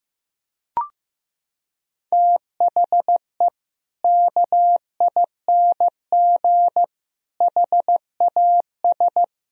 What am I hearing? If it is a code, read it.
Morse code, heard as a keyed tone, about 16 wpm, 700 Hz. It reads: THE KING HAS
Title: Sentences from Top 1000 Words - Review Part 2 - 15wpm